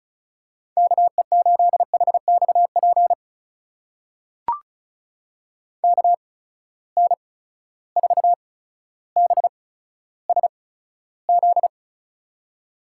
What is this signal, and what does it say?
Morse code, heard as a keyed tone, about 36 wpm, spaced as for 15 wpm, 700 Hz. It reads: KE8HXP KN4BSZ
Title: USA Call Signs - 6 Characters - 8x Character Spacing - 35wpm